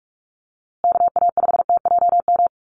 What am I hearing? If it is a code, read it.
Morse code, heard as a keyed tone, about 44 wpm, 700 Hz. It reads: KA5TJM
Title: USA Call Signs Repeated 3x 45wpm